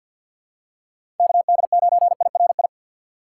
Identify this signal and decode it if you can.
Morse code, heard as a keyed tone, about 48 wpm, 700 Hz. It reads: KD9IRI
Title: USA Call Signs - 6x Character Spacing - 50wpm